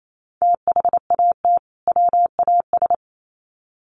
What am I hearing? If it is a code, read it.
Morse code, heard as a keyed tone, about 28 wpm, 700 Hz. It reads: THAT WAS
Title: Sets of 2 Words 28wpm